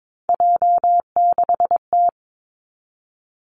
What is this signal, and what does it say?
Morse code, heard as a keyed tone, about 22 wpm, 700 Hz. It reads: J6T
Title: Three Challenging Letter-Number Combinations 22wpm